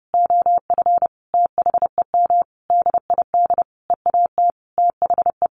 Morse code, heard as a keyed tone, about 30 wpm, 700 Hz. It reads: OF THEM DID EAT THE
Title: Sentences from Top 1500 Words 30wpm